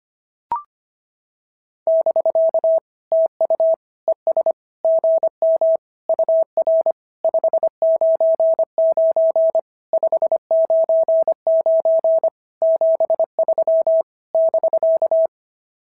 Morse code, heard as a keyed tone, about 25 wpm, 690 Hz. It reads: <BK> TU ES GM UR 599 599 73 <BK>
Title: SOTA QSO Practice 25wpm